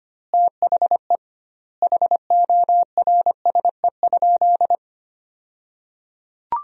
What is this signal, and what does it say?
Morse code, heard as a keyed tone, about 25 wpm, 700 Hz. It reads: THE HORSE?
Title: 2x Word-Spacing - Sentences from Top 500 Words - 25wpm